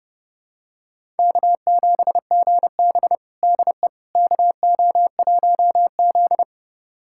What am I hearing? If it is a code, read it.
Morse code, heard as a keyed tone, about 31 wpm, 700 Hz. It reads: K7GB DE KO1Z
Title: USA Callsign DE Callsign - 4 Characters - 30wpm